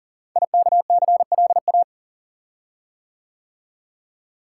Morse code, heard as a keyed tone, about 39 wpm, 700 Hz. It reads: IKCLA
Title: Code Groups - 5 Letter:Numbers - 40wpm